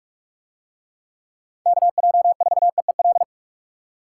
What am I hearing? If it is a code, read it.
Morse code, heard as a keyed tone, about 44 wpm, 700 Hz. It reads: KJ4EEL